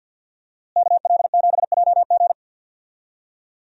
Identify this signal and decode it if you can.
Morse code, heard as a keyed tone, about 52 wpm, 700 Hz. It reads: KL7JG
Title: DXCC-100 Call Signs - 5 Characters - Repeated 2x - 50wpm